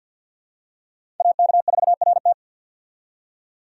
Morse code, heard as a keyed tone, about 52 wpm, 700 Hz. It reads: AK4RT
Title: USA Call Signs - 5 Characters - 50wpm